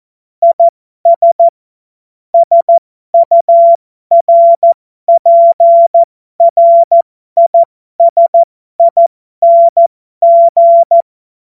Morse code, heard as a keyed tone, about 14 wpm, spaced as for 12 wpm, 680 Hz. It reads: IS SURPRISING
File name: Morsekurs / Witze